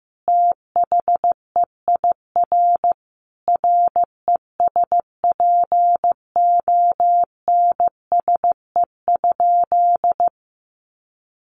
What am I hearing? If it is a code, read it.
Morse code, heard as a keyed tone, about 15 wpm, 700 Hz. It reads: THEIR RESPONSE?
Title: Sentences from Top 1400 Words 15wpm